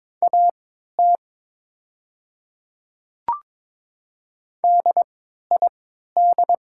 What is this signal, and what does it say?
Morse code, heard as a keyed tone, about 22 wpm, spaced as for 12 wpm, 700 Hz. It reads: AT DID
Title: 3x Character Spacing - Top 100 Words - 22wpm